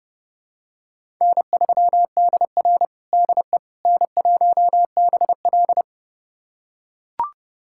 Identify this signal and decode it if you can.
Morse code, heard as a keyed tone, about 30 wpm, 700 Hz. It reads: N3DR DE N1BL
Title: USA Callsign DE Callsign - 4 Characters - 30wpm